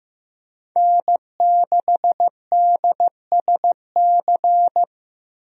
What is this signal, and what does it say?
Morse code, heard as a keyed tone, about 15 wpm, 700 Hz. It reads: N6DSC
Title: USA Call Signs - 5 Characters - Repeated 2x - 15wpm